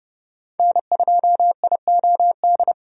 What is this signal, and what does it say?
Morse code, heard as a keyed tone, about 29 wpm, 700 Hz. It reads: N2IOD
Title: USA Call Signs - 5 Characters - 30wpm